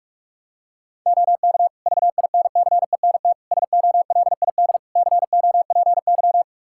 Morse code, heard as a keyed tone, about 46 wpm, 700 Hz. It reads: OK VINCENT SOLID COPY
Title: Practice QSO - Exchange 3 - More Info - 45wpm